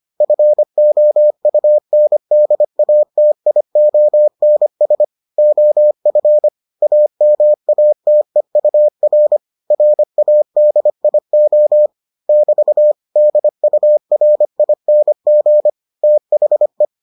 Morse code, heard as a keyed tone, about 25 wpm, 600 Hz. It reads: FOUNDATIONS OF AMATEUR RADIO <BT> DURING THE